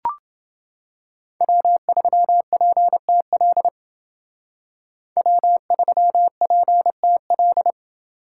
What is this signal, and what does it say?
Morse code, heard as a keyed tone, about 29 wpm, 700 Hz. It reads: W3PTL W3PTL